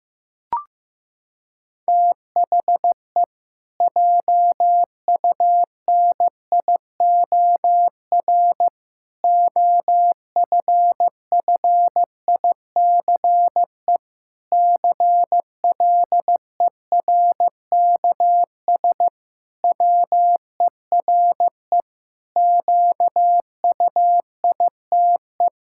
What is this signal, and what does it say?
Morse code, heard as a keyed tone, about 15 wpm, 700 Hz. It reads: THE JUNIOR OFFICE CLERKS WERE QUITE
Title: Pangram Challenge v2 - 15wpm